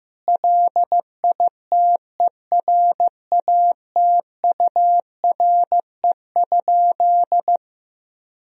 Morse code, heard as a keyed tone, about 15 wpm, 700 Hz. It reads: LITERATURE?
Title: Sentences from Top 1000 Words - Review Part 3 - 15wpm